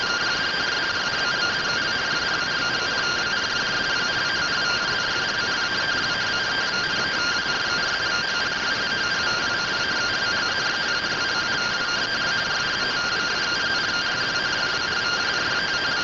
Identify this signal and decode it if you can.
Two-tone teleprinter signal, about 170 Hz shift.